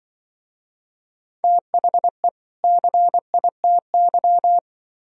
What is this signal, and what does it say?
Morse code, heard as a keyed tone, about 24 wpm, 700 Hz. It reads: THE CITY
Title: Speed-Racing - Sets of 2 Words 20wpm